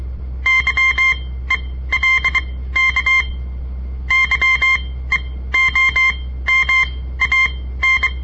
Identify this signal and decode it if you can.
Morse code, heard as a keyed tone, about 23 wpm, spaced as for 15 wpm, 2100 Hz. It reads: YELK YEOMAN
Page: On the air Morse code practice from the Southern Montana Amatuer Radio Association; all code, no read back